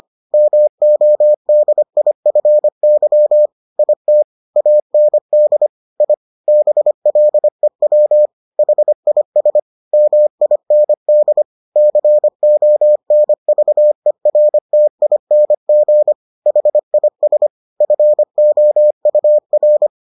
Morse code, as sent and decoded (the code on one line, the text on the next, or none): -- --- -.. .. ..-. -.-- / .. - / .- -. -.. / .. / -... .-.. .-- / .... .. ... / -- .. -. -.. / -.-. --- -. ...- . .-. - .. -. --. / .... .. ... / ..-. --- ..- .-.
MODIFY IT AND I BLEW HIS MIND CONVERTING HIS FOUR